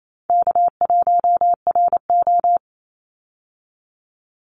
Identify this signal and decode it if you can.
Morse code, heard as a keyed tone, about 27 wpm, 700 Hz. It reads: K1RO
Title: Easy Call Signs Repeated 3x - 28wpm